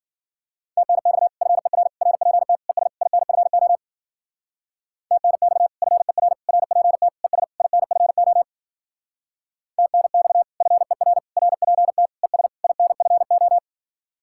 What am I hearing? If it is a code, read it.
Morse code, heard as a keyed tone, about 51 wpm, 700 Hz. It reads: TNX FER RPT ES INFO TNX FER RPT ES INFO TNX FER RPT ES INFO